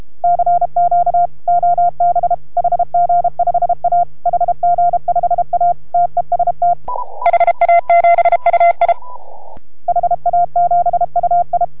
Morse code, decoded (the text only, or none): CQ OB HG5A HG5A TEST HA7UI HA7UI